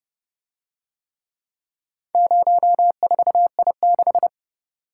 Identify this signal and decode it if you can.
Morse code, heard as a keyed tone, about 32 wpm, 700 Hz. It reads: 04I6